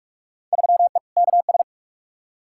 Morse code, heard as a keyed tone, about 44 wpm, 700 Hz. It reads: 3E KS
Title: Field Day Exchanges Encore - 45wpm